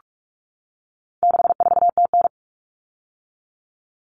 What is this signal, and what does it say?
Morse code, heard as a keyed tone, about 44 wpm, 700 Hz. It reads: 64TN
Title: All Cut Numbers - 4-Digit 45wpm